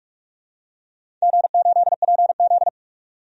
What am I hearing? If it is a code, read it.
Morse code, heard as a keyed tone, about 42 wpm, 700 Hz. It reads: G8PZ
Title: DXCC-100 Call Signs - 4 Characters - 45wpm